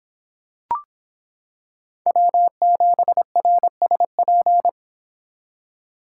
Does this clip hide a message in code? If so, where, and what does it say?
Yes: Morse code from 2.1 to 4.7 s, heard as a keyed tone, about 26 wpm, 700 Hz. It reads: W7RSP